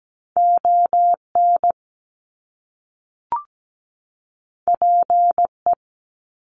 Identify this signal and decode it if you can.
Morse code, heard as a keyed tone, about 17 wpm, 700 Hz. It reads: ON PE
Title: Canadian Provinces and Territories 17wpm